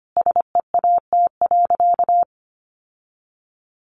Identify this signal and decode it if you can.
Morse code, heard as a keyed tone, about 25 wpm, 700 Hz. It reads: SEAT.
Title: Sentences from Top 1500 Words 25wpm